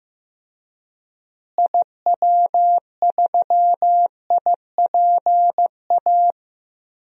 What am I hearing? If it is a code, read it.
Morse code, heard as a keyed tone, about 15 wpm, 700 Hz. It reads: IW3IPA
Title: DXCC-100 Call Signs - 6 Characters - Repeated 3x - 15wpm